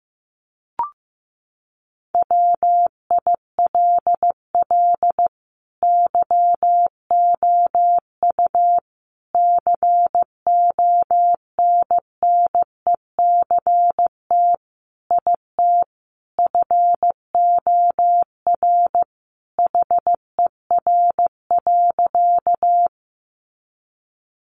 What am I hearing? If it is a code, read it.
Morse code, heard as a keyed tone, about 15 wpm, 700 Hz. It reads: WILL YOU CONNECT IT FOR HER.
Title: Sentences from Top 1500 Words 15wpm